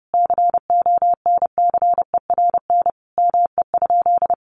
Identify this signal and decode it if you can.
Morse code, heard as a keyed tone, about 28 wpm, 700 Hz. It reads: CONCERN ME?
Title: Sentences from Top 1300 Words 30wpm